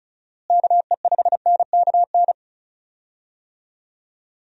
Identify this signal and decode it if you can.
Morse code, heard as a keyed tone, about 36 wpm, 700 Hz. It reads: KE5NKN
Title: USA Call Signs - 6 Characters - Repeated 3x - 35wpm